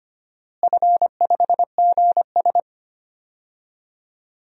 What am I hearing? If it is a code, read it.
Morse code, heard as a keyed tone, about 25 wpm, 700 Hz. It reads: F5GS
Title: DXCC-100 Call Signs - 4 Characters - Repeated 2x - 25wpm